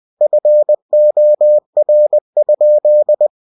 Morse code, heard as a keyed tone, about 20 wpm, 600 Hz. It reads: FOR?
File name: cw-books / winnie-the-pooh / winnie-20wpm-page-0005